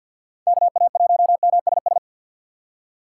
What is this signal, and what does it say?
Morse code, heard as a keyed tone, about 53 wpm, 700 Hz. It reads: KA1MSS
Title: USA Call Signs - 4x Character Spacing - 50wpm